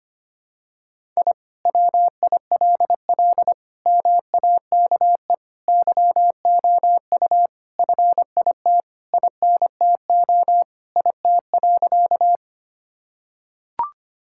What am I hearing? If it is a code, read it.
Morse code, heard as a keyed tone, about 25 wpm, 700 Hz. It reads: I WILL MAKE YOU FIT INTO IT.
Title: Sentences from Top 1000 Words - Review Part 5 - 25wpm